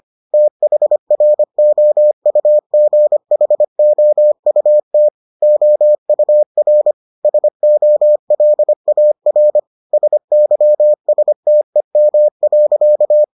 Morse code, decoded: THROUGHOUT OUR SOLAR SYSTEM.

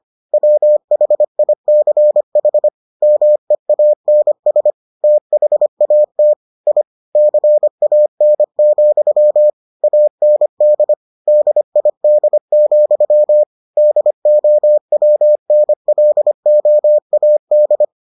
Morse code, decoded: WHICH MEANS THAT I CAN, AND DID, DOWNLOAD

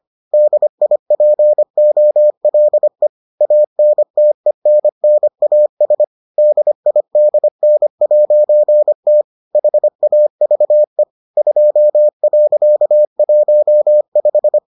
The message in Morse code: DIPOLE ANTENNAS DIDN'T HAVE 2.15

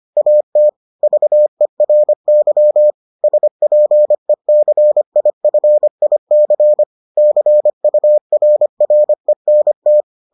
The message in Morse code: AT VERY SPECIFIC CURRENT

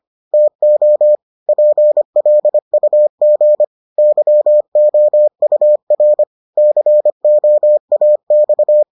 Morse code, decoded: TO PLUG YOUR COAX

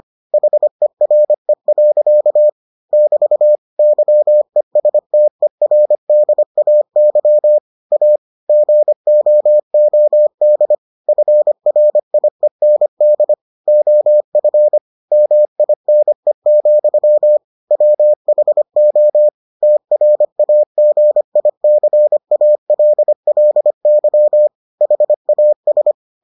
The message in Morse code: HERE. <BT> YESTERDAY A GOOD FRIEND OF MINE, WHO TRAGICALLY HAS

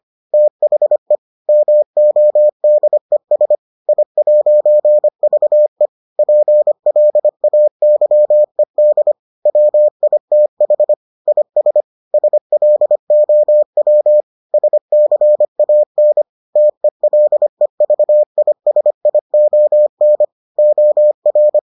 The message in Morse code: THE MODES I'VE PLAYED WITH IS SLOW SCAN TELEVISION OR

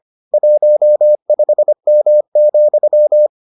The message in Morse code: 15M,

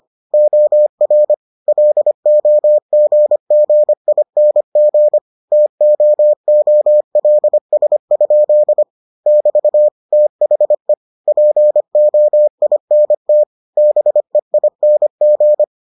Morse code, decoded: OR LOGGING TOOLS? <BT> THE POINT BEING

